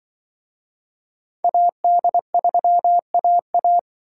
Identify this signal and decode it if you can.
Morse code, heard as a keyed tone, about 24 wpm, 700 Hz. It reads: AD3AA
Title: Speed Racing - USA 2x2 Call Signs 20wpm